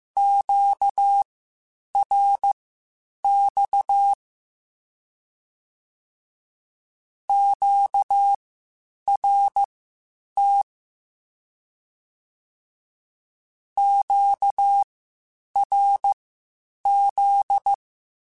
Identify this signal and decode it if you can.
Morse code, heard as a keyed tone, about 15 wpm, spaced as for 8 wpm, 800 Hz. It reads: QRX QRT QRZ
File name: Q-Calls Spaced